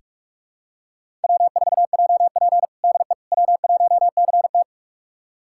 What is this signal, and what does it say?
Morse code, heard as a keyed tone, about 46 wpm, 700 Hz. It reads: W4JP DE W1CT
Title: USA Callsign DE Callsign - 4 Characters - 45wpm